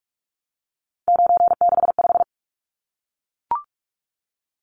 Morse code, heard as a keyed tone, about 44 wpm, 700 Hz. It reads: 965
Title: Numbers - 3-Digit 45wpm